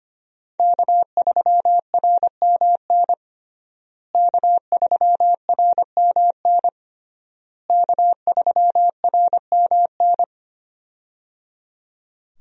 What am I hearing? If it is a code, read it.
Morse code, heard as a keyed tone, about 25 wpm, 700 Hz. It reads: K3RMN K3RMN K3RMN
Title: USA Call Signs - 5 Characters - Repeated 3x - 25wpm